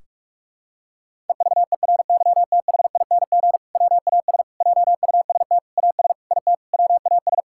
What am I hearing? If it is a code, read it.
Morse code, heard as a keyed tone, about 42 wpm, 700 Hz. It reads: EVERYTHING WAS JUST AS IT WAS
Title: Sentences from Top 1600 Words 45wpm